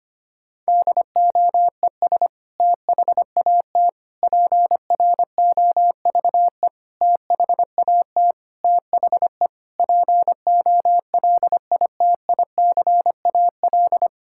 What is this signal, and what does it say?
Morse code, heard as a keyed tone, about 25 wpm, 700 Hz. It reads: DOES THAT PROVE THAT THE POLITICAL
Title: Sentences from Top 1600 Words 25wpm